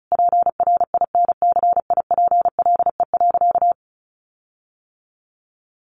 Morse code, heard as a keyed tone, about 34 wpm, 700 Hz. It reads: PRINCIPLE.
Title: Sentences from Top 1400 Words 35wpm